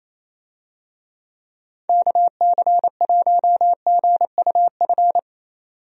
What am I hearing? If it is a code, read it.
Morse code, heard as a keyed tone, about 28 wpm, 700 Hz. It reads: KC1GUF
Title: USA Call Signs - 6 Characters - 28wpm